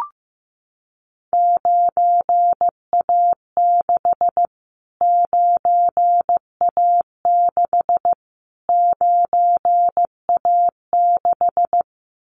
Morse code, heard as a keyed tone, about 15 wpm, 700 Hz. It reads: 9A6 9A6 9A6
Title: Callsign Prefixes for Top 25 DXCC - Repeated 3x - 15wpm